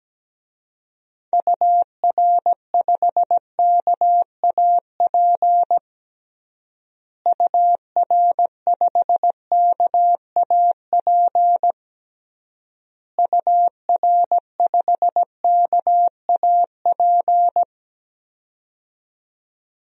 Morse code, heard as a keyed tone, about 17 wpm, 700 Hz. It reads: UR5KAP UR5KAP UR5KAP
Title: DXCC-100 Call Signs - 6 Characters - Repeated 3x - 17wpm